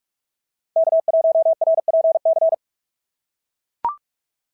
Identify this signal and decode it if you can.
Morse code, heard as a keyed tone, about 44 wpm, 690 Hz. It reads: K1RPC
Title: SOTA QSO Practice 45wpm